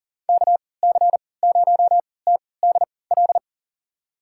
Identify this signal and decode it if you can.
Morse code, heard as a keyed tone, about 37 wpm, spaced as for 22 wpm, 700 Hz. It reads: KC0TDL
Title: USA Call Signs - 6 Characters - 3x Character Spacing - 40wpm